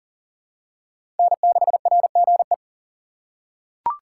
Morse code, heard as a keyed tone, about 40 wpm, 700 Hz. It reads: N6RGE
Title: USA Call Signs - 6x Character Spacing - 40wpm